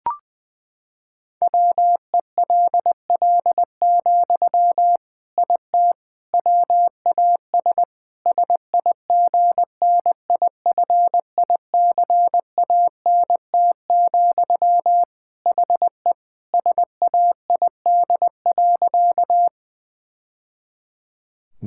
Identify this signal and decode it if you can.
Morse code, heard as a keyed tone, about 20 wpm, 700 Hz. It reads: WELL, IT WAS SIGNIFICANT, HE SAID.
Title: Sentences from Top 1000 Words - Review Part 6 - 20wpm